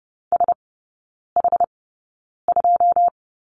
Morse code, heard as a keyed tone, about 30 wpm, spaced as for 9 wpm, 690 Hz. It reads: SH2
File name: Lesson 16e - Letter c - Three Characters 30wpm